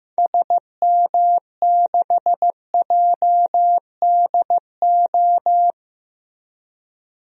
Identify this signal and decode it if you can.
Morse code, heard as a keyed tone, about 15 wpm, 700 Hz. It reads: SM6JDO
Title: DXCC-100 Call Signs - 6 Characters - Repeated 3x - 15wpm